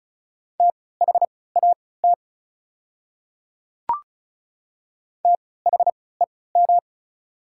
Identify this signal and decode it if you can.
Morse code, heard as a keyed tone, about 34 wpm, spaced as for 20 wpm, 690 Hz. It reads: THAT THEM
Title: 3x Character Spacing - Top 100 Words - 35wpm